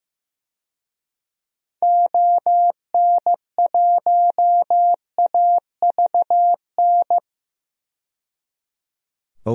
Morse code, heard as a keyed tone, about 15 wpm, 700 Hz. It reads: ON1AVN
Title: DXCC-100 Call Signs - 6 Characters - 15wpm